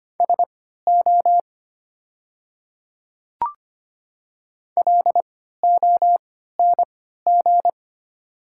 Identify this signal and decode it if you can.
Morse code, heard as a keyed tone, about 25 wpm, spaced as for 14 wpm, 690 Hz. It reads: SO LONG
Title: 3x Character Spacing - Top 100 Words - 25wpm